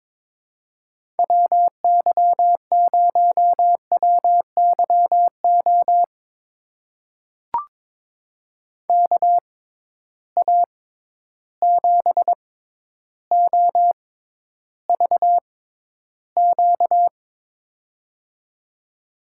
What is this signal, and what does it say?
Morse code, heard as a keyed tone, about 22 wpm, spaced as for 11 wpm, 700 Hz. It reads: WY0WYO KA7OVQ
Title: USA Call Signs - 6 Characters - 6x Character Spacing - 22wpm